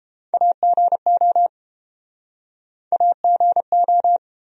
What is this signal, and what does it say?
Morse code, heard as a keyed tone, about 30 wpm, 700 Hz. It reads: AGO AGO